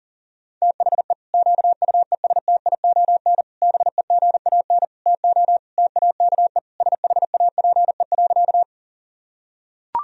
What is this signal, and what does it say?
Morse code, heard as a keyed tone, about 40 wpm, 700 Hz. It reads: THE QUESTION BEGAN TO TAKE SHAPE.
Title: Sentences from Top 1000 Words - Review Part 3 - 40wpm